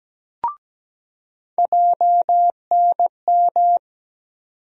Morse code, heard as a keyed tone, about 17 wpm, 700 Hz. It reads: JNM
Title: USA Call Sign Postfixes - 3 Letters 17wpm